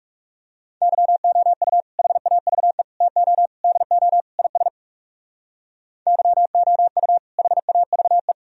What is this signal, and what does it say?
Morse code, heard as a keyed tone, about 37 wpm, 700 Hz. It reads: YOU HAVE TO DO IS YOU HAVE
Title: Speed-Racing - Sets of 5 Words 30wpm